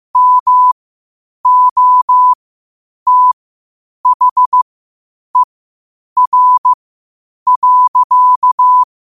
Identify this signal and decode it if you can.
Morse code, heard as a keyed tone, about 15 wpm, spaced as for 8 wpm, 1000 Hz. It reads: MOTHER.
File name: qotdfiles / 8wpm